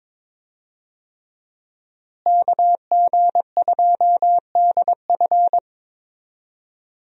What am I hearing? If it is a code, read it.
Morse code, heard as a keyed tone, about 22 wpm, 700 Hz. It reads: KG2DF
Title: USA Call Signs - 5 Characters - Repeated 3x - 22wpm